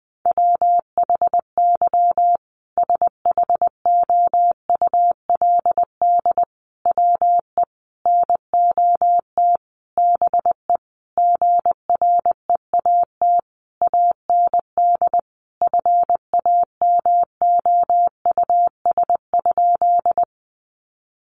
Morse code, heard as a keyed tone, about 20 wpm, 700 Hz. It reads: WHY SHOULD WE NOT BE GREAT AND FAMOUS?
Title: Sentences from Top 1500 Words 20wpm